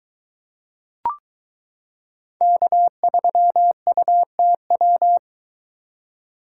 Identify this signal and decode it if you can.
Morse code, heard as a keyed tone, about 22 wpm, 700 Hz. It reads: K3UTW